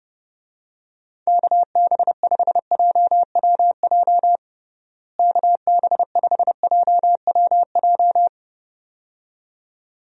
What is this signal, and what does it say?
Morse code, heard as a keyed tone, about 30 wpm, 700 Hz. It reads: KB5JWJ KB5JWJ
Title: USA Call Signs - 6 Characters - Repeated 2x - 30wpm